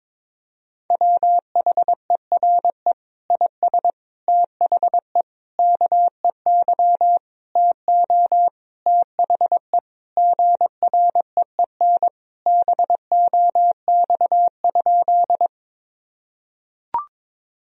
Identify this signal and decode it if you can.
Morse code, heard as a keyed tone, about 22 wpm, 700 Hz. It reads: WHERE IS THE KEY TO THE GREEN BOX?
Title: Sentences from Top 1000 Words - Review Part 4 - 22wpm